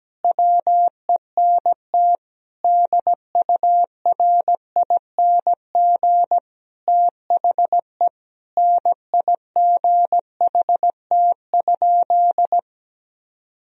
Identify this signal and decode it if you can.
Morse code, heard as a keyed tone, about 17 wpm, 700 Hz. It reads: WENT DURING THE NIGHT?
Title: Sentences from Top 1000 Words - Review Part 2 - 17wpm